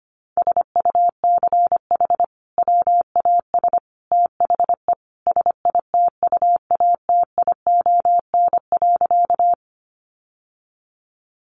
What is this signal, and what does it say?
Morse code, heard as a keyed tone, about 25 wpm, 700 Hz. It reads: SUCH WAS THE SITUATION.
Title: Sentences from Top 1200 Words 25wpm